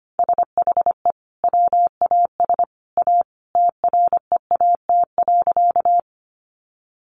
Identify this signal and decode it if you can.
Morse code, heard as a keyed tone, about 25 wpm, 700 Hz. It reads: SHE WAS A TREAT.
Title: Sentences from Top 1300 Words 25wpm